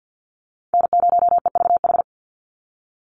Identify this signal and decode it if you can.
Morse code, heard as a keyed tone, about 50 wpm, 700 Hz. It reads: N0EVH
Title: NA SOTA Chaser Callsigns 50wpm